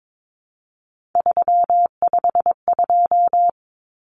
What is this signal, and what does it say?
Morse code, heard as a keyed tone, about 22 wpm, 700 Hz. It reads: 352
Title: Numbers - 3-Digit 22wpm